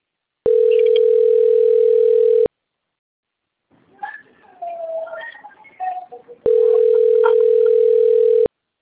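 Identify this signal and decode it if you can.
Ringback tone (North America), 440 + 480 Hz.
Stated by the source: Outgoing telephone call from Limeira.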